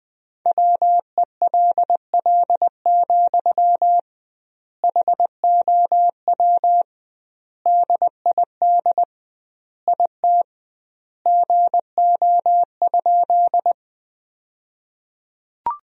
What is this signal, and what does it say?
Morse code, heard as a keyed tone, about 20 wpm, 700 Hz. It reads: WELL, HOW DID IT GO?
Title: 2x Word-Spacing - Sentences from Top 200 Words - 20wpm